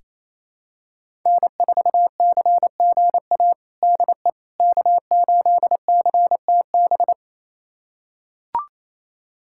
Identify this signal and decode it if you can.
Morse code, heard as a keyed tone, about 28 wpm, 700 Hz. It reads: N4CGA DE K8CTB
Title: USA Callsign DE Callsign - 5 Characters - 28wpm